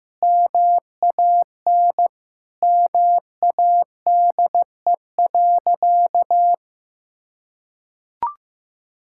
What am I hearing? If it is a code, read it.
Morse code, heard as a keyed tone, about 15 wpm, 700 Hz. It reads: MAN MADE.
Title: Sentences from Top 1000 Words - Review Part 6 - 15wpm